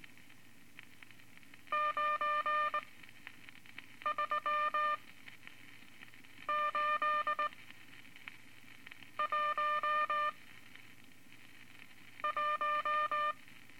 Morse code, heard as a keyed tone, about 21 wpm, spaced as for 5 wpm, 1200 Hz. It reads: 93811
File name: Morse Code Records / 14B LCM 157 Units V Groups of Five